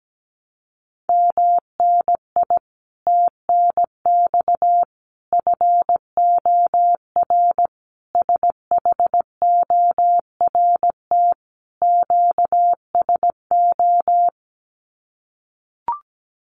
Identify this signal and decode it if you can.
Morse code, heard as a keyed tone, about 17 wpm, 700 Hz. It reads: MNI TNX FOR SHORT QSO
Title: Common QSO Phrases 17wpm